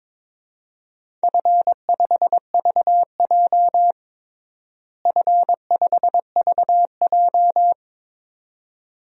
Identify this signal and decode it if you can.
Morse code, heard as a keyed tone, about 22 wpm, 700 Hz. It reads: F5VJ F5VJ
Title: DXCC-100 Call Signs - 4 Characters - Repeated 2x - 22wpm